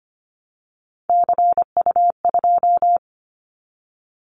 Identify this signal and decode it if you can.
Morse code, heard as a keyed tone, about 25 wpm, 700 Hz. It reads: CU2
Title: Callsign Prefixes for Top 100 DXCC - 25wpm